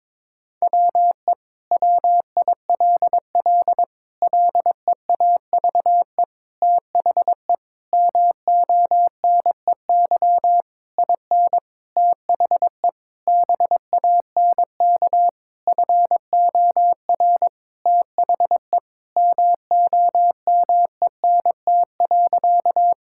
Morse code, heard as a keyed tone, about 22 wpm, 700 Hz. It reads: WE WILL LEAVE THE MONEY IN THE BANK FOR THE MOMENT.
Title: Sentences from Top 1000 Words - Review Part 6 - 22wpm